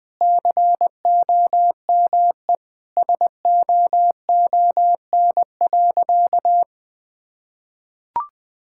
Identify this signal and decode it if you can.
Morse code, heard as a keyed tone, about 21 wpm, 700 Hz. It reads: COME SOON.
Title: Sentences with 2-Word Phrases - 20wpm